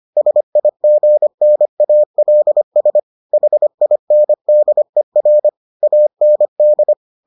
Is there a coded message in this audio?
Yes: Morse code from 0.2 to 6.9 s, heard as a keyed tone, about 25 wpm, 600 Hz. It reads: SIGNALS HINDER AND